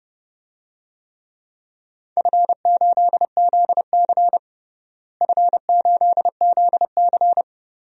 Morse code, heard as a keyed tone, about 31 wpm, 700 Hz. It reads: F8ZC F8ZC
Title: DXCC-100 Call Signs - 4 Characters - Repeated 3x - 30wpm